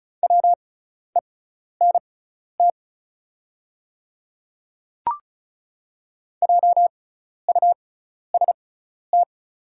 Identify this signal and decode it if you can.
Morse code, heard as a keyed tone, about 36 wpm, spaced as for 12 wpm, 690 Hz. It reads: WENT JUST
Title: 6x Character Spacing - Top 300 Words - 35wpm